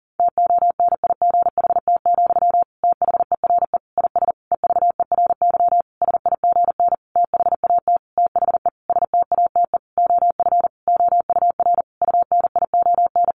Morse code, heard as a keyed tone, about 40 wpm, 700 Hz. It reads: TONIGHT, THERE IS EVERY SIGN THAT THE STATE OF OUR UNION
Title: Sentences from Top 1400 Words 40wpm